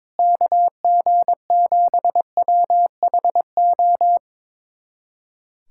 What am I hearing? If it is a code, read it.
Morse code, heard as a keyed tone, about 22 wpm, 700 Hz. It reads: KG7WHO
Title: USA Callsign DE Callsign 22wpm